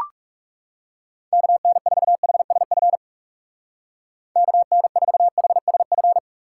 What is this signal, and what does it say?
Morse code, heard as a keyed tone, about 36 wpm, 700 Hz. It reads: KN4HSF KN4HSF
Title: Speed Racing - USA 2x3 Call Signs 30wpm